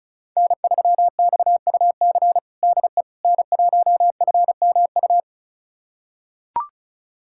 Morse code, heard as a keyed tone, about 36 wpm, 700 Hz. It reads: N3XUC DE N1FMU
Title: USA Callsign DE Callsign - 5 Characters - 35wpm